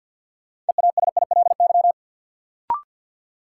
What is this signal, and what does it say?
Morse code, heard as a keyed tone, about 52 wpm, 700 Hz. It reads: EASILY
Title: Sentences from Top 900 Words - New Word Drill - 50wpm